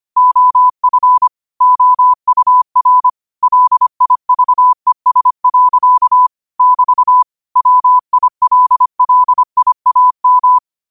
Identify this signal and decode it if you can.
Morse code, heard as a keyed tone, about 25 wpm, 1000 Hz. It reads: OF OUR LIVES. <BT> WILLIAM